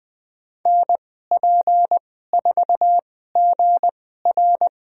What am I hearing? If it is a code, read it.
Morse code, heard as a keyed tone, about 20 wpm, spaced as for 14 wpm, 700 Hz. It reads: NP4GR
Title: USA Call Signs - 5 Characters - 2x Character Spacing - 20wpm